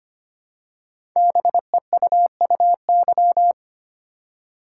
Morse code, heard as a keyed tone, about 25 wpm, 700 Hz. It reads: BEUUY
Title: Code Groups - 5 Letters - 25wpm